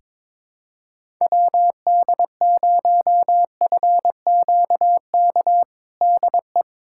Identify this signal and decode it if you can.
Morse code, heard as a keyed tone, about 21 wpm, 700 Hz. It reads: WD0FQK DE
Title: USA Callsign DE Callsign 22wpm